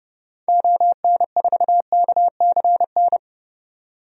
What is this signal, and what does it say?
Morse code, heard as a keyed tone, about 30 wpm, 700 Hz. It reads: ON4KCN
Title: DXCC-100 Call Signs - 6 Characters - 30wpm